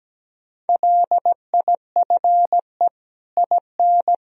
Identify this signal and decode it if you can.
Morse code, heard as a keyed tone, about 17 wpm, 700 Hz. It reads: LIFE IN
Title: Sentences from Top 1000 Words - Review Part 3 - 17wpm